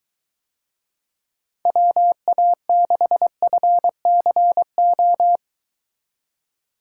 Morse code, heard as a keyed tone, about 22 wpm, 700 Hz. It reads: WA6FCO